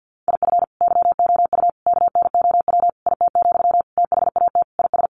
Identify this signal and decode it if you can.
Morse code, heard as a keyed tone, about 50 wpm, 700 Hz. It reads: IF YOU KNOW IT, THAT IS
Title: Sentences from Top 400 Words 50wpm